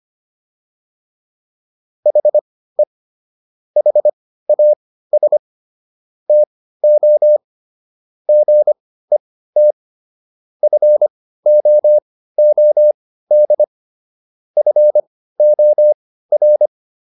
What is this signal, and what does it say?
Morse code, heard as a keyed tone, about 25 wpm, spaced as for 15 wpm, 600 Hz. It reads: HE HAS TO GET FOOD FOR